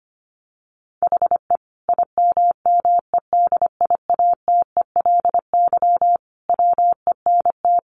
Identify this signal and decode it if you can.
Morse code, heard as a keyed tone, about 25 wpm, 700 Hz. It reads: HE IMMEDIATELY WENT
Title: Sentences from Top 1100 Words 25wpm